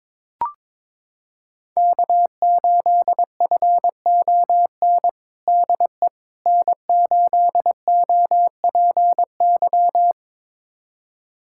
Morse code, heard as a keyed tone, about 22 wpm, 700 Hz. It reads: K8FON DE N8OPY
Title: USA Callsign DE Callsign - 5 Characters - 22wpm